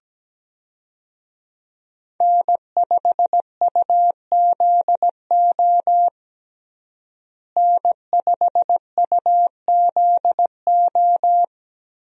Morse code, heard as a keyed tone, about 17 wpm, 700 Hz. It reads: N5UZO N5UZO
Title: USA Call Signs - 5 Characters - Repeated 2x - 17wpm